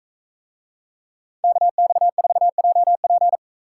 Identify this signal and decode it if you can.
Morse code, heard as a keyed tone, about 44 wpm, 700 Hz. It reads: KX4JP